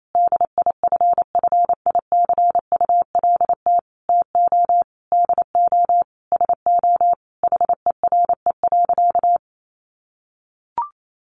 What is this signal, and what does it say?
Morse code, heard as a keyed tone, about 28 wpm, 700 Hz. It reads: DIFFICULT TO DO SO HERE.
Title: Sentences from Top 600 Words 28wpm